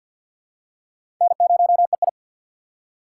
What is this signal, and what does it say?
Morse code, heard as a keyed tone, about 46 wpm, 700 Hz. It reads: N0EI
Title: USA Call Signs - 4 Characters - 6x Character Spacing - 50wpm